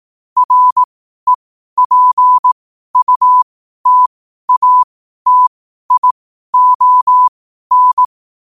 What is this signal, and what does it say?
Morse code, heard as a keyed tone, about 18 wpm, spaced as for 12 wpm, 1000 Hz. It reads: REPUTATION
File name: qotdfiles / 12wpm